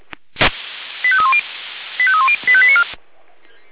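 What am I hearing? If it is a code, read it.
Five-tone selective call (ZVEI), heard as quick runs of five tones, tones of about 80 ms. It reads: 85310 85310 85894
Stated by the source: ZVEI five-tone selective call received by SDR, address 85894.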